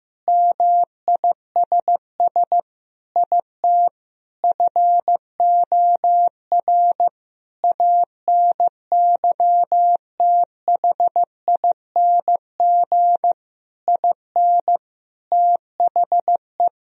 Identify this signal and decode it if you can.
Morse code, heard as a keyed tone, about 15 wpm, 700 Hz. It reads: MISS IT FOR ANYTHING IN THE
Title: Sentences from Top 1000 Words - Review Part 3 - 15wpm